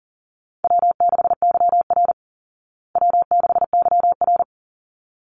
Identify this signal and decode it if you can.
Morse code, heard as a keyed tone, about 40 wpm, 700 Hz. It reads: W6YR W6YR
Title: Call Signs Repeated 3x - 40wpm